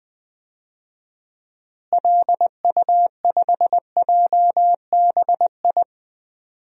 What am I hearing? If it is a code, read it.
Morse code, heard as a keyed tone, about 20 wpm, 700 Hz. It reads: LU5JBI
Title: DXCC-100 Call Signs - 6 Characters - Repeated 3x - 20wpm